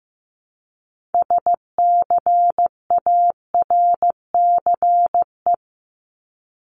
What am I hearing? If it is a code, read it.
Morse code, heard as a keyed tone, about 15 wpm, 700 Hz. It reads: SCARCE
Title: LY Suffix - Top 100 Words with Base Words 15wpm